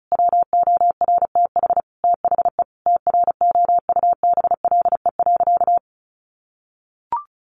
Morse code, heard as a keyed tone, about 34 wpm, 700 Hz. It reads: WORTH THE TROUBLE.
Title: Sentences from Top 1500 Words 35wpm